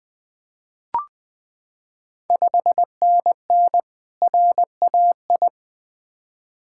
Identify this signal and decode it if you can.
Morse code, heard as a keyed tone, about 20 wpm, 690 Hz. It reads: 5NN RAI
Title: Ontario QSO Party Exchange 20wpm